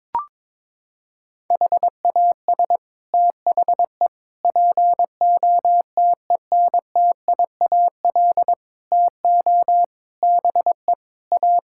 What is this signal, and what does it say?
Morse code, heard as a keyed tone, about 22 wpm, 700 Hz. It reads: HAS THE POTENTIAL TO BE A
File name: practice / 22wpm / Sets of 6 Words - 22wpm